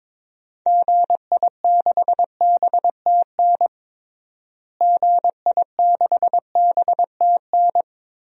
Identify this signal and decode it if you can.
Morse code, heard as a keyed tone, about 22 wpm, 700 Hz. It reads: GI6BTN GI6BTN
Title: DXCC-100 Call Signs - 6 Characters - Repeated 3x - 22wpm